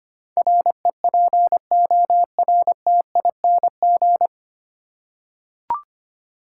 Morse code, heard as a keyed tone, about 25 wpm, 700 Hz. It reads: REPORTING
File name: practice / 25wpm / 9 Letter Words - First 3 Letters Given - 25wpm